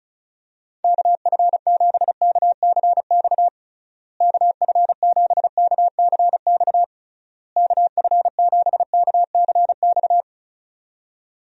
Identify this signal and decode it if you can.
Morse code, heard as a keyed tone, about 34 wpm, 700 Hz. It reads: KF7KCX KF7KCX KF7KCX
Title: USA Call Signs - 6 Characters - Repeated 3x - 35wpm